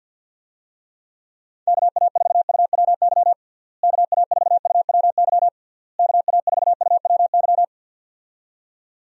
Morse code, heard as a keyed tone, about 52 wpm, 700 Hz. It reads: KA4UWY KA4UWY KA4UWY
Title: USA Call Signs - 6 Characters - Repeated 3x - 50wpm